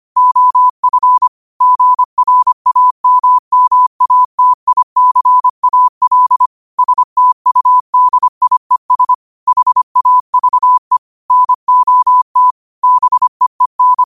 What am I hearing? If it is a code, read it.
Morse code, heard as a keyed tone, about 25 wpm, 1000 Hz. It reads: OF GRAMMATICAL STUDIES HAVE NOT BEEN
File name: qotdfiles / 25wpm